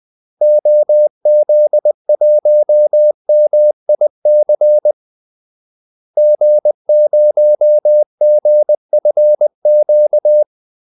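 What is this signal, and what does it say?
Morse code, heard as a keyed tone, about 20 wpm, 600 Hz. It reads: OZ1MIC G0GFQ
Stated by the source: Morse call-sign practice at 20 wpm.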